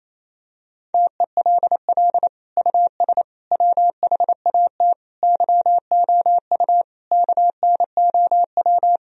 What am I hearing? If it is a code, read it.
Morse code, heard as a keyed tone, about 28 wpm, 700 Hz. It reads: TELL US WHAT YOU KNOW
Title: Sentences from Top 1000 Words - Review Part 4 - 28wpm